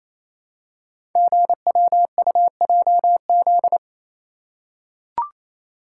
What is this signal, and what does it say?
Morse code, heard as a keyed tone, about 29 wpm, 700 Hz. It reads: GWUJZ
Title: Code Groups - 5 Letters - 28wpm